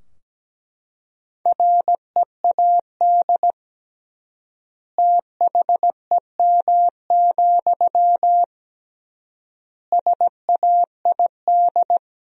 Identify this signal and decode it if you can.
Morse code, heard as a keyed tone, about 17 wpm, 700 Hz. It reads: READ THEM, SAID
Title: 3x Word-Spacing - Sentences from Top 400 Words - 17wpm